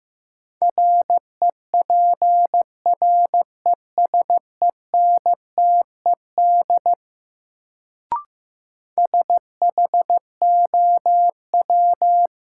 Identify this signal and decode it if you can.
Morse code, heard as a keyed tone, about 15 wpm, 700 Hz. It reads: REPRESENTED SHOW
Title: Sets of 5 Words - New Word Drill - 15wpm